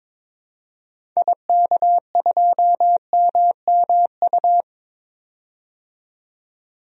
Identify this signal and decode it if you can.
Morse code, heard as a keyed tone, about 22 wpm, 700 Hz. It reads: IK2MMU
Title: DXCC-100 Call Signs - 6 Characters - 22wpm